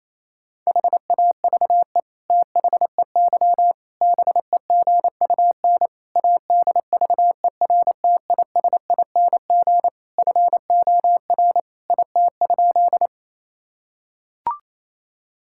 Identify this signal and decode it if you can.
Morse code, heard as a keyed tone, about 28 wpm, 700 Hz. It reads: HAVE THEY BEGUN ADVERTISING FOR IT?
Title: Sentences from Top 1000 Words - Review Part 6 - 28wpm